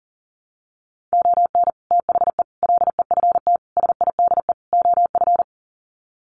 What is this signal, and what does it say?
Morse code, heard as a keyed tone, about 40 wpm, 700 Hz. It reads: ON THE LEFT SIDE OF
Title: Sets of 5 Words Encore2 40wpm